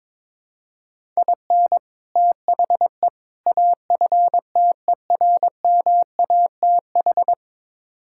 Morse code, heard as a keyed tone, about 22 wpm, 700 Hz. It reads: IN THE AFTERMATH
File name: practice / 22wpm / Build A Phrase - 5-Word Phrases - 22wpm